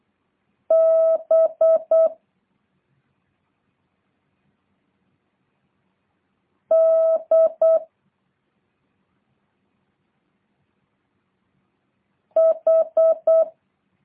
Morse code, heard as a keyed tone, about 8 wpm, 650 Hz. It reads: B D H